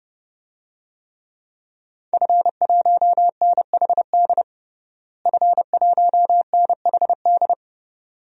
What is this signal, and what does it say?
Morse code, heard as a keyed tone, about 30 wpm, 700 Hz. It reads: F1NHD F1NHD
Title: DXCC-100 Call Signs - 5 Characters - Repeated 2x - 30wpm